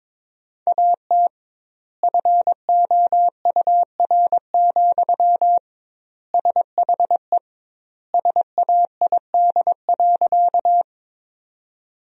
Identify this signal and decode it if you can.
Morse code, heard as a keyed tone, about 22 wpm, 700 Hz. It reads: AT FOUR, SHE SAID.
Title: 2x Word-Spacing - Sentences from Top 300 Words - 22wpm